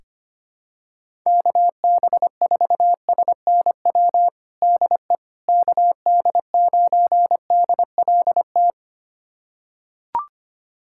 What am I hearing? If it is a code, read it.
Morse code, heard as a keyed tone, about 25 wpm, 700 Hz. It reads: KB4SNW DE KD9DLT
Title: USA Callsign DE Callsign 25wpm